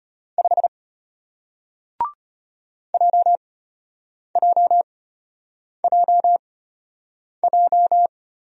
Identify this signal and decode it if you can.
Morse code, heard as a keyed tone, about 33 wpm, spaced as for 8 wpm, 700 Hz. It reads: 5 JJJJ